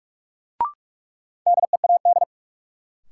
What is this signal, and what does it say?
Morse code, heard as a keyed tone, about 46 wpm, 700 Hz. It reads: DEAD
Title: Sentences from Top 1500 Words - New Word Drill Alt - 45wpm